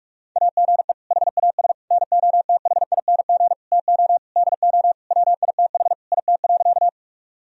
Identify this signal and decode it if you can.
Morse code, heard as a keyed tone, about 47 wpm, 700 Hz. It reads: AGE HAS NOTHING TO DO WITH IT.
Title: Sentences from Top 1000 Words - Review Part 3 - 45wpm